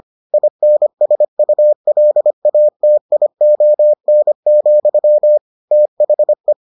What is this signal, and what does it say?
Morse code, heard as a keyed tone, about 25 wpm, 600 Hz. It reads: INSULATION, THE